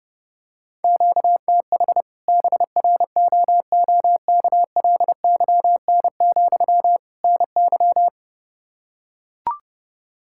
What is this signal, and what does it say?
Morse code, heard as a keyed tone, about 30 wpm, 700 Hz. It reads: QTH BROOKLYN, NY
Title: Practice QSO - Exchange 1 - Beginning - 30wpm